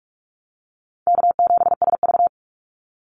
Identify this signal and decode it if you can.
Morse code, heard as a keyed tone, about 44 wpm, 700 Hz. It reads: K7SV
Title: Easy Call Signs - 45wpm